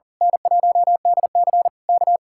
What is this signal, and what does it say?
Morse code, heard as a keyed tone, about 38 wpm, 700 Hz. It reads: N1DC K
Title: Speed-Racing Common QSO Phrases Encore - 40wpm